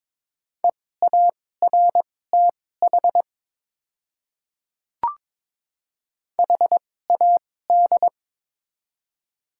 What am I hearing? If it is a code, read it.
Morse code, heard as a keyed tone, about 21 wpm, spaced as for 16 wpm, 700 Hz. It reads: EARTH HAD
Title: 2x Character Spacing - Top 300 Words - 22wpm